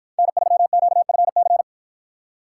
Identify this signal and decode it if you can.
Morse code, heard as a keyed tone, about 53 wpm, 700 Hz. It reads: N3QFC